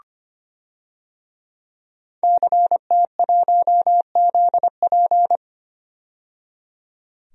Morse code, heard as a keyed tone, about 25 wpm, 700 Hz. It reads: CT1ZP
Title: DXCC-100 Call Signs - 5 Characters - 25wpm